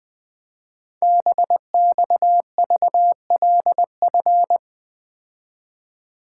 Morse code, heard as a keyed tone, about 20 wpm, 700 Hz. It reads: BXVLF
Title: Code Groups - 5 Letters - 20wpm